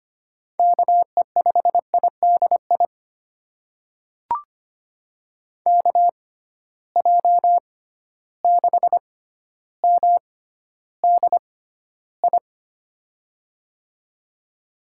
Morse code, heard as a keyed tone, about 25 wpm, spaced as for 13 wpm, 700 Hz. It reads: KE5IDI KJ6MDI
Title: USA Call Signs - 6 Characters - 6x Character Spacing - 25wpm